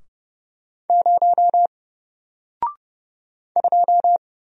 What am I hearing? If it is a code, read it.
Morse code, heard as a keyed tone, about 30 wpm, 700 Hz. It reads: 0 2
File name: Single-Letter-Number-30wpm